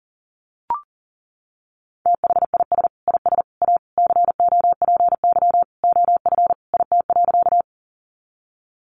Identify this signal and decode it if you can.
Morse code, heard as a keyed tone, about 40 wpm, 700 Hz. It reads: THIS IS A COPY OF IT.
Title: Sentences from Top 1400 Words 40wpm